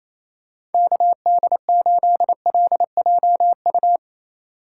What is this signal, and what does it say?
Morse code, heard as a keyed tone, about 28 wpm, 700 Hz. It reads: KD8LJU
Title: USA Call Signs - 6 Characters - 8x Character Spacing - 28wpm